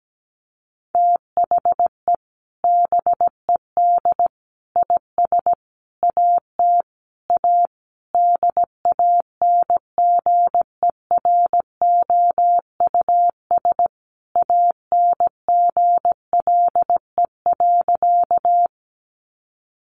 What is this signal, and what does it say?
Morse code, heard as a keyed tone, about 17 wpm, 700 Hz. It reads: THE BED IS AT A DANGEROUS ANGLE.
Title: Sentences from Top 1500 Words 17wpm